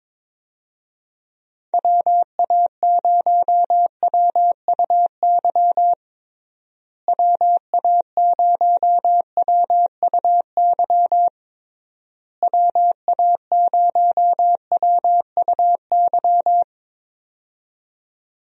Morse code, heard as a keyed tone, about 22 wpm, 700 Hz. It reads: WA0WUY WA0WUY WA0WUY
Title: USA Call Signs - 6 Characters - Repeated 3x - 22wpm